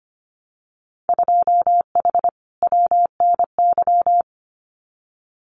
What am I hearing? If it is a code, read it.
Morse code, heard as a keyed tone, about 25 wpm, 700 Hz. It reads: 2H WNY
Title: Winter Field Day Exchanges Encore 25wpm